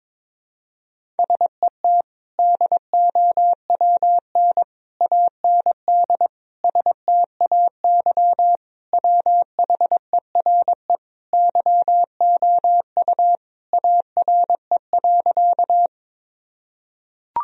Morse code, heard as a keyed tone, about 21 wpm, 700 Hz. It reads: SET DOWN AND STAY WHERE YOU ARE.
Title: Sentences from Top 1000 Words - Review Part 1 - 22wpm